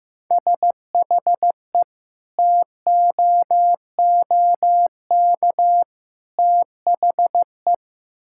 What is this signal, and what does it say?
Morse code, heard as a keyed tone, about 16 wpm, 700 Hz. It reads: SHE TOOK THE
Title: Sentences from Top 1000 Words - Review Part 6 - 15wpm